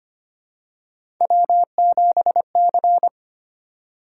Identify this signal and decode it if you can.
Morse code, heard as a keyed tone, about 25 wpm, 700 Hz. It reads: W7C
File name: Three Letter-Number Combinations - 25wpm